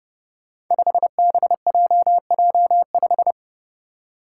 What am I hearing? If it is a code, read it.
Morse code, heard as a keyed tone, about 30 wpm, 700 Hz. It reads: 5BJJ5
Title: Five Challenging Letter-Number Combinations 30wpm